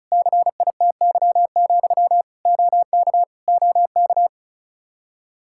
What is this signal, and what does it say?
Morse code, heard as a keyed tone, about 36 wpm, 700 Hz. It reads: CITY, OK OK
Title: Practice QSO - Exchange 1 - Beginning - 35wpm